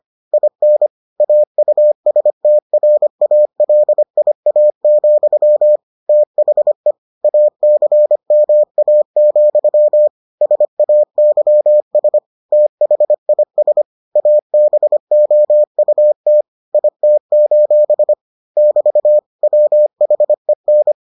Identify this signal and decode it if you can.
Morse code, heard as a keyed tone, about 25 wpm, 600 Hz. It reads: IN AUSTRALIA, THE ACMA, SAYS THIS ABOUT IT: <BT> WHEN